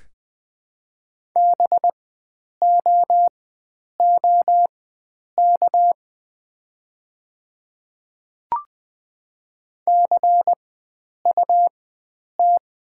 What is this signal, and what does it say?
Morse code, heard as a keyed tone, about 20 wpm, spaced as for 9 wpm, 700 Hz. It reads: BOOK CUT
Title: 4x Character Spacing - Top 500 Words - 20wpm